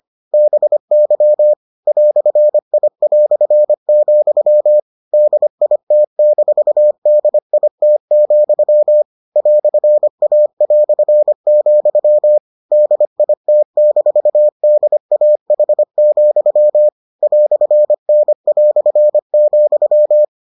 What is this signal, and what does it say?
Morse code, heard as a keyed tone, about 25 wpm, 600 Hz. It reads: BY "I", DIT-DIT, "A", DIT-DAH, "N",